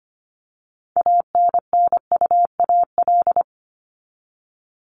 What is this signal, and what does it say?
Morse code, heard as a keyed tone, about 25 wpm, 700 Hz. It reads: ANNUAL
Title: Top 1251-1500 Words 25wpm